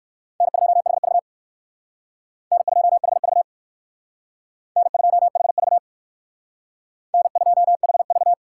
Morse code, heard as a keyed tone, about 62 wpm, 700 Hz. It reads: N2HV N2HV N2HV N2HV